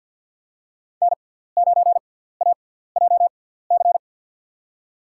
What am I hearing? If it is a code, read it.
Morse code, heard as a keyed tone, about 52 wpm, spaced as for 17 wpm, 700 Hz. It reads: N9AJC
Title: USA Call Signs - 5 Characters - 6x Character Spacing - 50wpm